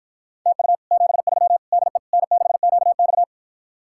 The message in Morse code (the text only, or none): TU 73 DE N6QX